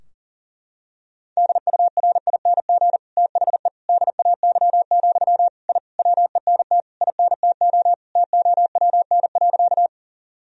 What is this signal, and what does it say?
Morse code, heard as a keyed tone, about 37 wpm, 700 Hz. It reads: DURING THE DAY, I WENT INTO TOWN.
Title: Sentences from Top 1000 Words - Review Part 2 - 40wpm